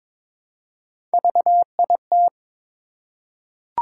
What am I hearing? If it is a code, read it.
Morse code, heard as a keyed tone, about 22 wpm, 700 Hz. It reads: VIT
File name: practice / 22wpm / Three Letter-Number Combinations - 22wpm